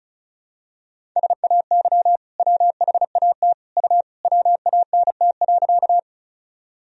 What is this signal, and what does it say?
Morse code, heard as a keyed tone, about 36 wpm, 700 Hz. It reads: SAY WHAT U WANT.
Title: Sentences from Top 200 Words and Abbreviations 35wpm